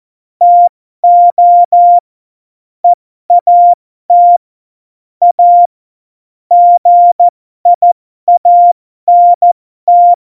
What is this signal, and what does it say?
Morse code, heard as a keyed tone, about 14 wpm, spaced as for 12 wpm, 700 Hz. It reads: TO EAT A GIANT
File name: Morsekurs / Witze